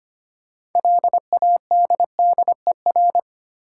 Morse code, heard as a keyed tone, about 25 wpm, 700 Hz. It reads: LADDER
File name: practice / 25wpm / Difficult Words with Hints - 25wpm